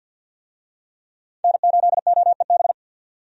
Morse code, heard as a keyed tone, about 51 wpm, 700 Hz. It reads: N8OEB
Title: USA Call Signs - 5 Characters - 50wpm